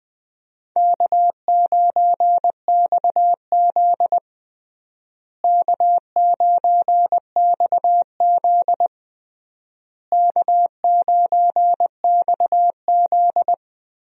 Morse code, heard as a keyed tone, about 21 wpm, 700 Hz. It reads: K9XZ K9XZ K9XZ
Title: USA Call Signs - 4 Characters - Repeated 3x - 20wpm